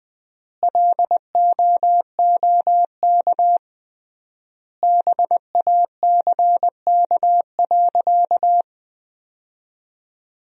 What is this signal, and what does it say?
Morse code, heard as a keyed tone, about 21 wpm, 700 Hz. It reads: LOOK BACK.
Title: 3x Word-Spacing - Sentences from Top 300 Words - 20wpm